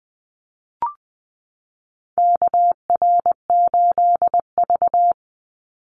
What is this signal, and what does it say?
Morse code, heard as a keyed tone, about 20 wpm, 700 Hz. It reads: KR8V
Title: Call Signs - 20wpm